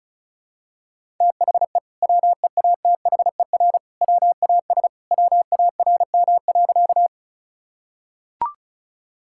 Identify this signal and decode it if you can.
Morse code, heard as a keyed tone, about 36 wpm, 700 Hz. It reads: THE WEATHER WAS WARM.
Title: Sentences from Top 1000 Words - Review Part 5 - 35wpm